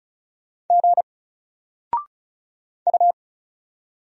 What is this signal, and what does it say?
Morse code, heard as a keyed tone, about 36 wpm, 700 Hz. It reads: G U
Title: Single Letter-Number 35wpm